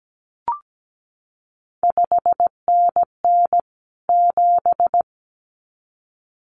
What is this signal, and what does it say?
Morse code, heard as a keyed tone, about 17 wpm, 700 Hz. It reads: 5NN 7
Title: CQ WW DX Contest Exchange Encore - 17wpm